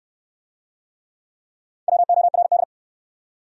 Morse code, heard as a keyed tone, about 68 wpm, 700 Hz. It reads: WORD